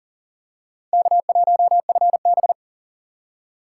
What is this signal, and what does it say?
Morse code, heard as a keyed tone, about 40 wpm, 700 Hz. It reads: K1FB
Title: USA Call Signs - 4 Characters - 6x Character Spacing - 40wpm